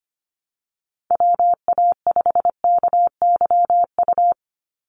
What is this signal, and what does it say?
Morse code, heard as a keyed tone, about 25 wpm, 700 Hz. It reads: WA5KYU
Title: USA Call Signs 25wpm